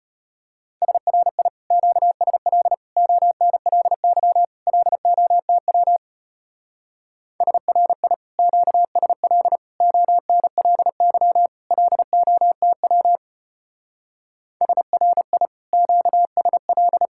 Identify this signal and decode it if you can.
Morse code, heard as a keyed tone, about 33 wpm, 700 Hz. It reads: SRI QSL ONLY LOTW SRI QSL ONLY LOTW SRI QSL